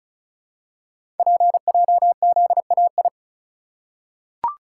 Morse code, heard as a keyed tone, about 36 wpm, 700 Hz. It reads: PJZAI
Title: Code Groups - 5 Letters - 35wpm